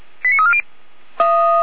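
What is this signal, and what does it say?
Five-tone selective call (ZVEI), heard as quick runs of five tones, tones of about 70 ms. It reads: 89340
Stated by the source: Five-tone ZVEI selective call received by SDR, from the fire brigade (Feuerwehr) Orlamünde recording server.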